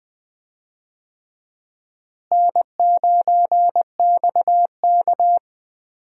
Morse code, heard as a keyed tone, about 20 wpm, 700 Hz. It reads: N9XK
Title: USA Call Signs - 4 Characters - Repeated 3x - 20wpm